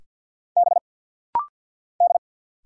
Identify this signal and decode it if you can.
Morse code, heard as a keyed tone, about 50 wpm, 700 Hz. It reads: B D
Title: Single Letters - Rapid-Fire 50wpm